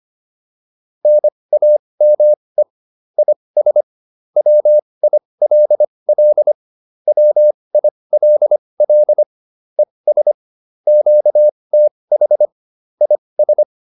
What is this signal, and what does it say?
Morse code, heard as a keyed tone, about 25 wpm, spaced as for 20 wpm, 600 Hz. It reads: NAME IS WILL WILL ES QTH IS